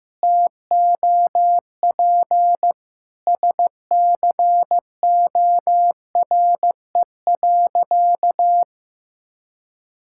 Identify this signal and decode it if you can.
Morse code, heard as a keyed tone, about 15 wpm, 700 Hz. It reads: TOP SCORE.
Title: Sentences from Top 1600 Words 15wpm